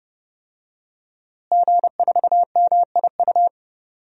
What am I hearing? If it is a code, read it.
Morse code, heard as a keyed tone, about 32 wpm, 700 Hz. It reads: G4MIU
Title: DXCC-100 Call Signs - 5 Characters - Repeated 2x - 30wpm